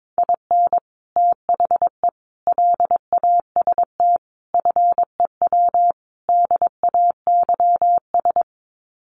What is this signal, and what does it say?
Morse code, heard as a keyed tone, about 22 wpm, 700 Hz. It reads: IN THE LAST FEW DAYS
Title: Sets of 5 Words Encore2 22wpm